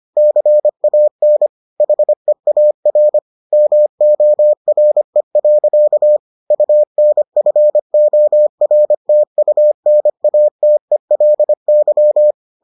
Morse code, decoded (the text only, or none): CAN HEAR MORE. UNFORTUNATELY